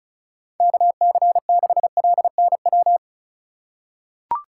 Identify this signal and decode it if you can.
Morse code, heard as a keyed tone, about 34 wpm, 690 Hz. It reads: KC6LNW
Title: USA Call Signs - 6 Characters - 3x Character Spacing - 35wpm